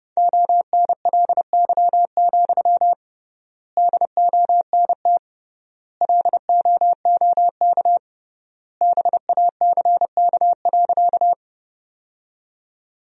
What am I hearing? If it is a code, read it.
Morse code, heard as a keyed tone, about 30 wpm, 700 Hz. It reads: ONLY, DONT LOOK BACK.
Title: 3x Word-Spacing - Sentences from Top 300 Words - 30wpm